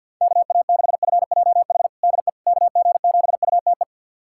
Morse code, heard as a keyed tone, about 47 wpm, 700 Hz. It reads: KA6FJH DE KG7UTE
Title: USA Callsign DE Callsign - 6 Characters - 50wpm